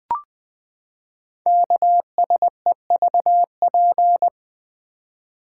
Morse code, heard as a keyed tone, about 20 wpm, 700 Hz. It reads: KSEVP
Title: Code Groups - 5 Letter:Numbers - 20wpm